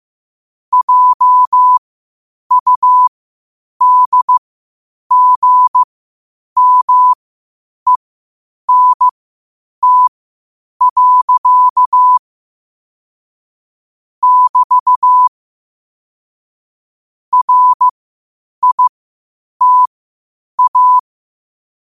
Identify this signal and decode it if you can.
Morse code, heard as a keyed tone, about 15 wpm, spaced as for 8 wpm, 1000 Hz. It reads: JUDGMENT. <BT> RITA
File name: qotdfiles / 8wpm